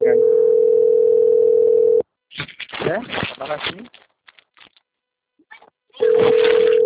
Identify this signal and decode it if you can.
Ringback tone (North America), 440 + 480 Hz, with speech.
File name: out- - - - - .17548